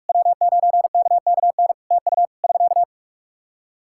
Morse code, heard as a keyed tone, about 44 wpm, 700 Hz. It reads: W9KKN TU <SK>
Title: Practice QSO - Exchange 4 - Ending - 45wpm